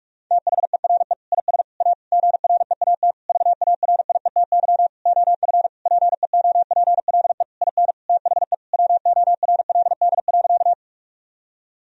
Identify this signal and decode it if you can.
Morse code, heard as a keyed tone, about 46 wpm, 700 Hz. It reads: THERE IS A GREAT VARIETY OF PEOPLE IN THE WORLD.
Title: Sentences from Top 1000 Words - Review Part 4 - 45wpm